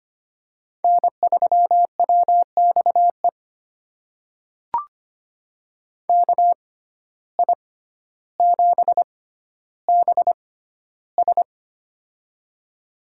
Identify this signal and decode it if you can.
Morse code, heard as a keyed tone, about 25 wpm, spaced as for 13 wpm, 700 Hz. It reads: N3WXE KI7BS
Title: USA Call Signs - 5 Characters - 6x Character Spacing - 25wpm